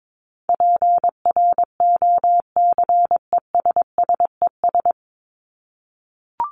Morse code, heard as a keyed tone, about 22 wpm, 700 Hz. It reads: PROCESSES
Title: ES:S Suffix - Top 100 Words with Base Words 22wpm